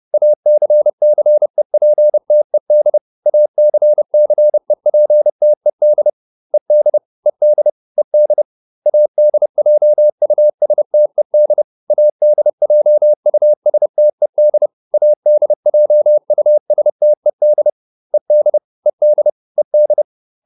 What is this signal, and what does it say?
Morse code, heard as a keyed tone, about 30 wpm, 600 Hz. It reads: ACCEPTED ACCEPTED ED ED ED ADJUSTED ADJUSTED ADJUSTED ED ED ED